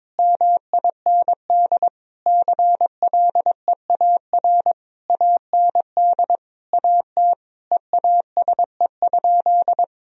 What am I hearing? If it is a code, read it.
Morse code, heard as a keyed tone, about 22 wpm, 700 Hz. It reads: MIND CLEAR AND AT EASE?
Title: Sentences from Top 1000 Words - Review Part 1 - 22wpm